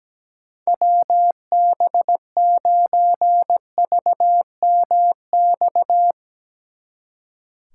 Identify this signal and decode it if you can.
Morse code, heard as a keyed tone, about 20 wpm, 700 Hz. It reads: WB9VMX
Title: Speed Racing - USA 2x3 Call Signs 17wpm